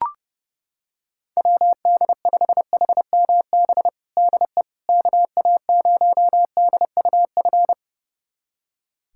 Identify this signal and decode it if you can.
Morse code, heard as a keyed tone, about 30 wpm, 700 Hz. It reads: WD5HMB DE KA0DUF
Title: USA Callsign DE Callsign 30wpm